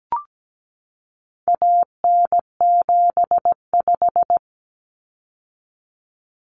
Morse code, heard as a keyed tone, about 17 wpm, 700 Hz. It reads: AN75
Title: Common Cut Numbers - 4-Digit 17wpm